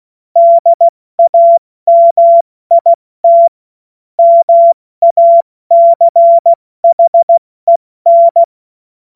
Morse code, heard as a keyed tone, about 16 wpm, 680 Hz. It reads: DAMIT MACHEN
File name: Morsekurs / Witze / Witz 11. Juli 2023 70 F80